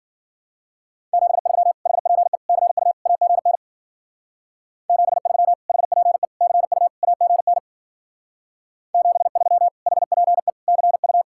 Morse code, heard as a keyed tone, about 51 wpm, 700 Hz. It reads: 73 HPE CU AGN 73 HPE CU AGN 73 HPE CU